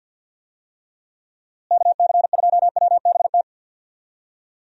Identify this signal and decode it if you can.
Morse code, heard as a keyed tone, about 51 wpm, 700 Hz. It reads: KC2WBT
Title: USA Call Signs - 6 Characters - Repeated 3x - 50wpm